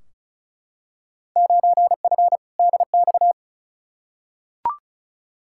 Morse code, heard as a keyed tone, about 34 wpm, 700 Hz. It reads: 9F DX
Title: Field Day Exchanges Encore - 35wpm